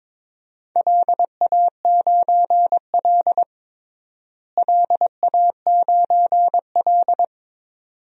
Morse code, heard as a keyed tone, about 22 wpm, 700 Hz. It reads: LA9L LA9L
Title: DXCC-100 Call Signs - 4 Characters - Repeated 2x - 22wpm